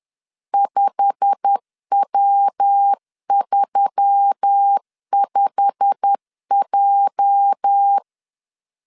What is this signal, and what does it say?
Morse code, heard as a keyed tone, about 10 wpm, 800 Hz. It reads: 5W35J
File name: LEZIONI / 8 YJ45 / 8 YJ45 10wpm 2549